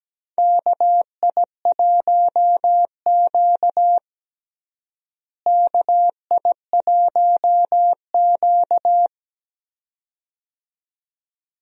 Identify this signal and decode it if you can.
Morse code, heard as a keyed tone, about 17 wpm, 700 Hz. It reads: KI1Q KI1Q
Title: USA Call Signs - 4 Characters - Repeated 2x - 17wpm